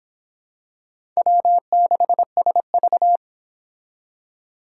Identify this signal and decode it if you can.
Morse code, heard as a keyed tone, about 26 wpm, 700 Hz. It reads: W6SV